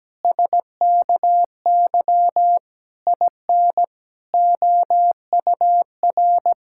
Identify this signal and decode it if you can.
Morse code, heard as a keyed tone, about 17 wpm, 700 Hz. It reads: SKY IN OUR
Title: Pangram Challenge v2 - 17wpm